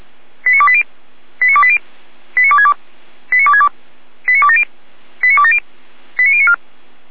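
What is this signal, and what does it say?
Five-tone selective call (ZVEI), heard as quick runs of five tones, tones of about 70 ms. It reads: 89280 89280 89262 89262 89270 89270 89094
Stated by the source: Five-tone ZVEI selective call received on SDR.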